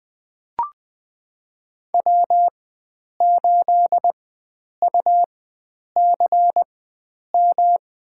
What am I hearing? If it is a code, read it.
Morse code, heard as a keyed tone, about 20 wpm, spaced as for 9 wpm, 700 Hz. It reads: W8UCM
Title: USA Call Signs - 5 Characters - 4x Character Spacing - 20wpm